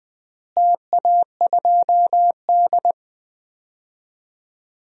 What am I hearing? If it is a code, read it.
Morse code, heard as a keyed tone, about 21 wpm, 700 Hz. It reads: TA2D
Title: DXCC-100 Call Signs - 4 Characters - Repeated 3x - 20wpm